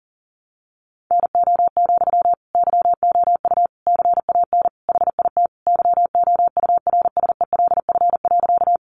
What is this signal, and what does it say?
Morse code, heard as a keyed tone, about 40 wpm, 700 Hz. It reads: NO, YOU CAN HIT YOURSELF.
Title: Sentences from Top 1200 Words 40wpm